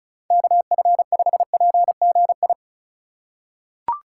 Morse code, heard as a keyed tone, about 34 wpm, 700 Hz. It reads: KF5PGI
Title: USA Call Signs - 6 Characters - 4x Character Spacing - 35wpm